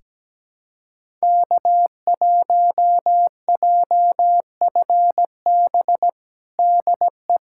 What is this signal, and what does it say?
Morse code, heard as a keyed tone, about 17 wpm, 700 Hz. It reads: K1JFB DE
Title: USA Callsign DE Callsign 17wpm